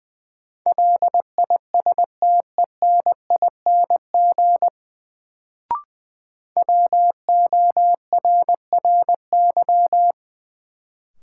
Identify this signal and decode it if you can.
Morse code, heard as a keyed tone, about 20 wpm, 700 Hz. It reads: LISTENING WORRY
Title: Sets of 5 Words - New Word Drill - 20wpm